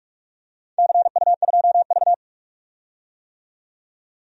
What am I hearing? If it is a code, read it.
Morse code, heard as a keyed tone, about 46 wpm, 690 Hz. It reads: CU2V
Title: DXCC-100 Call Signs - 4 Characters - 45wpm